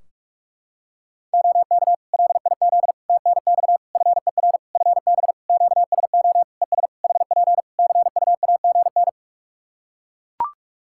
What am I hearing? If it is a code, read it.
Morse code, heard as a keyed tone, about 41 wpm, 700 Hz. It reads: OK LIZ TNX FER FB QSO ES HP CUAGN
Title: Practice QSO - Exchange 4 - Ending - 45wpm